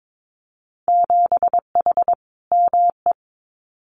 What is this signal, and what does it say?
Morse code, heard as a keyed tone, about 22 wpm, 700 Hz. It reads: 7H ME
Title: Winter Field Day Exchanges Encore 22wpm